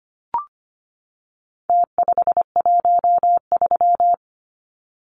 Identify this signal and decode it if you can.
Morse code, heard as a keyed tone, about 25 wpm, 700 Hz. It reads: T513
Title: Numbers - Contest Style - 4-Digit 25wpm